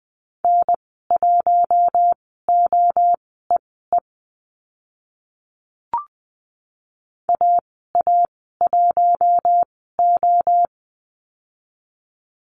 Morse code, heard as a keyed tone, about 20 wpm, spaced as for 14 wpm, 700 Hz. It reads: N1OEE AA1O
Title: 048 Lesson 9g - Number 4 - Call Signs 20wpm